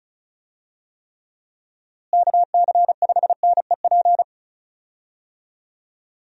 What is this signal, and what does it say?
Morse code, heard as a keyed tone, about 36 wpm, 700 Hz. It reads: KC5NEP
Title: USA Call Signs - 6 Characters - Repeated 2x - 35wpm